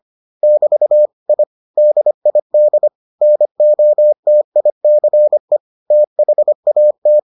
Morse code, heard as a keyed tone, about 25 wpm, 600 Hz. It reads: <BT> I DID NOTICE THAT